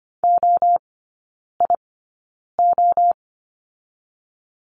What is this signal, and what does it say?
Morse code, heard as a keyed tone, about 25 wpm, spaced as for 9 wpm, 690 Hz. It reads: OIO
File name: Lesson 14e - Number 2 - Three Characters 25wpm